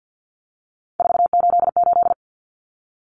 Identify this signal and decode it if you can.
Morse code, heard as a keyed tone, about 50 wpm, 700 Hz. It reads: 488
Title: Numbers - 3-Digit 50wpm